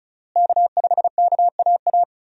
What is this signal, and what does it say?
Morse code, heard as a keyed tone, about 34 wpm, 700 Hz. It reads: K5KAA
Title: USA Call Signs - 5 Characters - Repeated 2x - 35wpm